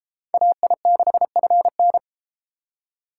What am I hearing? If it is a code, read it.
Morse code, heard as a keyed tone, about 29 wpm, 700 Hz. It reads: AI6FN